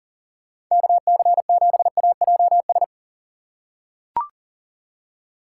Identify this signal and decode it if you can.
Morse code, heard as a keyed tone, about 40 wpm, 700 Hz. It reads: KC7AJS
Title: USA Call Signs - 6 Characters - 4x Character Spacing - 40wpm